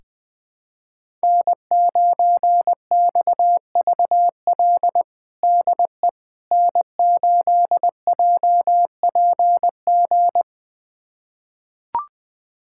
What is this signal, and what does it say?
Morse code, heard as a keyed tone, about 21 wpm, 700 Hz. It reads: N9XVL DE N8JPG
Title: USA Callsign DE Callsign - 5 Characters - 20wpm